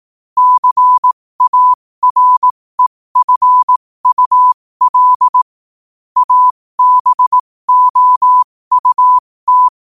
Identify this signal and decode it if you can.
Morse code, heard as a keyed tone, about 18 wpm, spaced as for 15 wpm, 1000 Hz. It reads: CAREFUL ABOUT
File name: qotdfiles / 15wpm